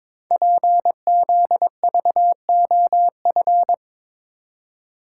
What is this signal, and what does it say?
Morse code, heard as a keyed tone, about 22 wpm, 700 Hz. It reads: PZVOF
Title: Code Groups - 5 Letters - 22wpm